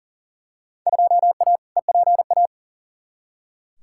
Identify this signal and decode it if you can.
Morse code, heard as a keyed tone, about 40 wpm, 700 Hz. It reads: 2A EPA
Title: Field Day Exchanges - 40wpm